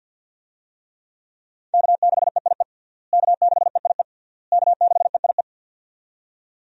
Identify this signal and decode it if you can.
Morse code, heard as a keyed tone, about 52 wpm, 700 Hz. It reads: K6EIE K6EIE K6EIE
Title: USA Call Signs - 5 Characters - Repeated 3x - 50wpm